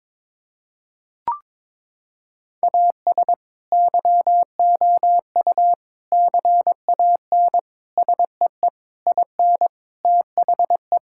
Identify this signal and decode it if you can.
Morse code, heard as a keyed tone, about 22 wpm, 700 Hz. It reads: AS YOU CAN SEE IN THE
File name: Sentences with 7-Word Phrases - 22wpm